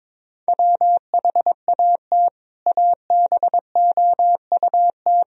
Morse code, heard as a keyed tone, about 22 wpm, 700 Hz. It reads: WHAT ABOUT
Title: Sentences from Top 1000 Words - Review Part 1 - 22wpm